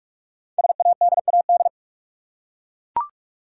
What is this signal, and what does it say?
Morse code, heard as a keyed tone, about 44 wpm, 690 Hz. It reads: SADAD
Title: Code Groups - 5 Letters - 45wpm